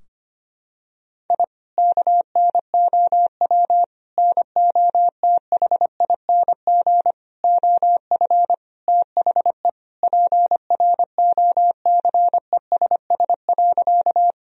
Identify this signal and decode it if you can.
Morse code, heard as a keyed tone, about 25 wpm, 700 Hz. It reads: I KNOW NOTHING OF THE PROCESS.
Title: Sentences from Top 1000 Words - Review Part 3 - 25wpm